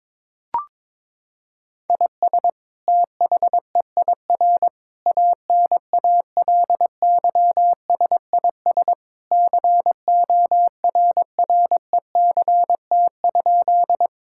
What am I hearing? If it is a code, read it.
Morse code, heard as a keyed tone, about 22 wpm, 700 Hz. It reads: IS THEIR ANALYSIS CORRECT?
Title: Sentences from Top 1000 Words - Review Part 5 - 22wpm